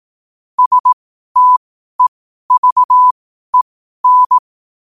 Morse code, heard as a keyed tone, about 18 wpm, spaced as for 12 wpm, 1000 Hz. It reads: STEVEN
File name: qotdfiles / 12wpm